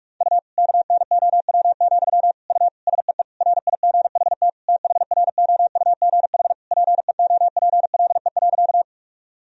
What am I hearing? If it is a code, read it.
Morse code, heard as a keyed tone, about 41 wpm, 700 Hz. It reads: U KNOW, U SEE RIGHT THROUGH PEOPLE.
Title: Sentences from Top 200 Words and Abbreviations 45wpm